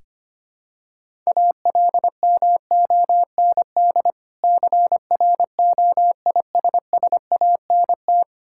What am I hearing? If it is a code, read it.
Morse code, heard as a keyed tone, about 25 wpm, 690 Hz. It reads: ALMOND CROISSANT
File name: practice / 25wpm / Difficult Words with Hints - 25wpm